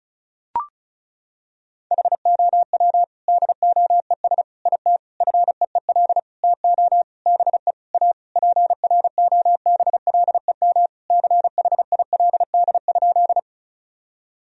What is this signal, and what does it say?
Morse code, heard as a keyed tone, about 34 wpm, 700 Hz. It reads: HOW DOES IT FEEL TO BE A PROBLEM CHILD?
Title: Sentences from Top 1000 Words - Review Part 2 - 35wpm